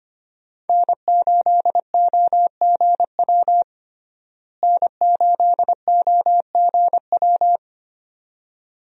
Morse code, heard as a keyed tone, about 25 wpm, 700 Hz. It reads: N8OGW N8OGW
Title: USA Call Signs - 5 Characters - Repeated 2x - 25wpm